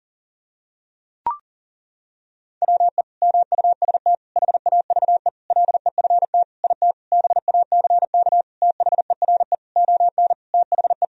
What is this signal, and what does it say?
Morse code, heard as a keyed tone, about 40 wpm, 700 Hz. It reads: WE MUST HAVE LEFT IT BACK THERE ON THE
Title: Sentences from Top 1000 Words - Review Part 2 - 40wpm